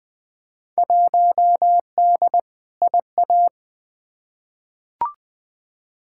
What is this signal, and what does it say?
Morse code, heard as a keyed tone, about 20 wpm, 700 Hz. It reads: 1D IA
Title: Field Day Exchanges - 20wpm